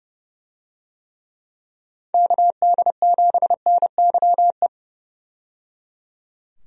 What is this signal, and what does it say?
Morse code, heard as a keyed tone, about 30 wpm, 700 Hz. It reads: KD7NYE
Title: USA Call Signs - 6 Characters - 30wpm